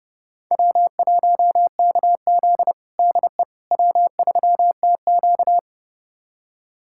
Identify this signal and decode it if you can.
Morse code, heard as a keyed tone, about 31 wpm, 700 Hz. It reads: W1KZ DE W3TQ
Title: USA Callsign DE Callsign - 4 Characters - 30wpm